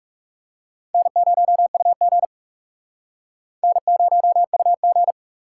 Morse code, heard as a keyed tone, about 37 wpm, 700 Hz. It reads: N0UG N0UG
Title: Speed Racing - USA 1x2 Call Signs 30wpm